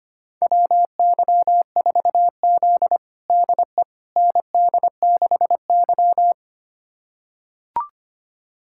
Morse code, heard as a keyed tone, about 25 wpm, 700 Hz. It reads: WY4Z DE ND6Y
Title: USA Callsign DE Callsign - 4 Characters - 25wpm